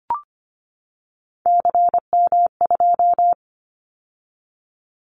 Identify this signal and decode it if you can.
Morse code, heard as a keyed tone, about 25 wpm, 700 Hz. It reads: CM2
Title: Callsign Prefixes for Top 100 DXCC - 25wpm